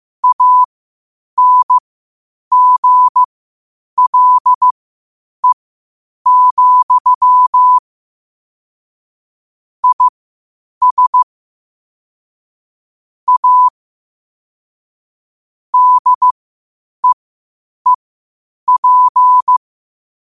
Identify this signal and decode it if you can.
Morse code, heard as a keyed tone, about 15 wpm, spaced as for 8 wpm, 1000 Hz. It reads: ANGLE, IS A DEEP
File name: qotdfiles / 8wpm